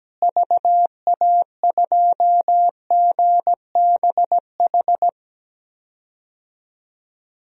Morse code, heard as a keyed tone, about 17 wpm, 700 Hz. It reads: VA2GBH
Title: DXCC-100 Call Signs - 6 Characters - Repeated 3x - 17wpm